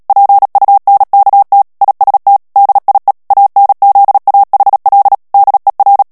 Morse code, heard as a keyed tone, about 37 wpm, 800 Hz. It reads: PUNKT IST DIE ANZAHL DER
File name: morse mp3 / ALLEMAND texte clair 2 vitesse 35 mots minute